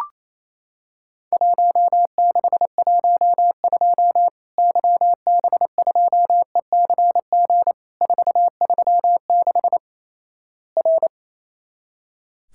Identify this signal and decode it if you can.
Morse code, heard as a keyed tone, about 28 wpm, 700 Hz. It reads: 1612 YB2ECG 436 R
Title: Worked All Europe DX Contest - QTC Exchange - 28wpm